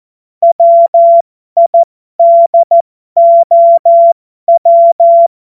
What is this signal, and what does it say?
Morse code, heard as a keyed tone, about 14 wpm, spaced as for 12 wpm, 680 Hz. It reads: WIDOW
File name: Morsekurs / Witze